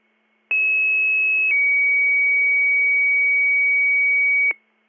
Two-tone page, heard as a one-second tone, then a three-second tone, about 2600 then 2250 Hz. It is heard from 0.5 to 4.5 s.